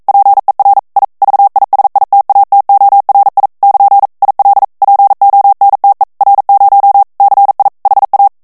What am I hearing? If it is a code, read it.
Morse code, heard as a keyed tone, about 42 wpm, 800 Hz. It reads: PER I VISITATORI <KN> IL PONTE R0 CI HA
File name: morse mp3 / ITALIEN texte clair 1 vitesse 40 mots minute